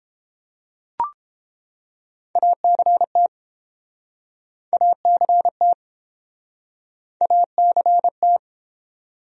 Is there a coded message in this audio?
Yes: Morse code from 2.4 to 8.4 s, heard as a keyed tone, about 30 wpm, 700 Hz. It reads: ACT ACT ACT